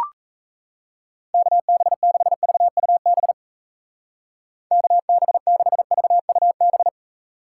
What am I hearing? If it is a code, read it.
Morse code, heard as a keyed tone, about 36 wpm, 700 Hz. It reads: KB6VUB KB6VUB